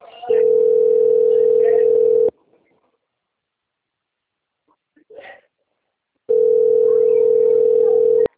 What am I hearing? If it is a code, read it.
Ringback tone (North America), 440 + 480 Hz.